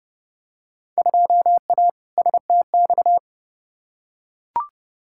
Morse code, heard as a keyed tone, about 32 wpm, 700 Hz. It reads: 2A STX
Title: Field Day Exchanges - 30wpm